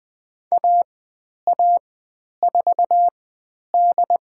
Morse code, heard as a keyed tone, about 20 wpm, spaced as for 10 wpm, 700 Hz. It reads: AA4D
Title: 102 Lesson 17g - Letter u - Call Signs 20wpm